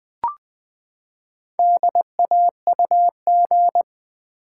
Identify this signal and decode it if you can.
Morse code, heard as a keyed tone, about 21 wpm, 700 Hz. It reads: DAUG
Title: Build A Word - 4 Letters At A Time - 8-Letter Words - 20wpm